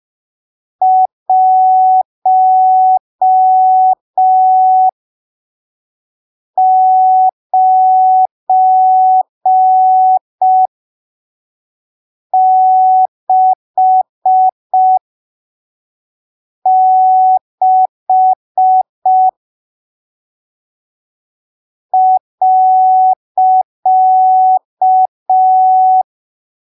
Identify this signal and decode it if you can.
Morse code, heard as a keyed tone, about 5 wpm, spaced as for 3 wpm, 750 Hz. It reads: 1966 .
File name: arquivos / texto / texto 07 05ppm